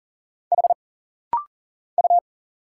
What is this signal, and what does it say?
Morse code, heard as a keyed tone, about 40 wpm, 700 Hz. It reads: H U
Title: Single Letters - Rapid-Fire 40wpm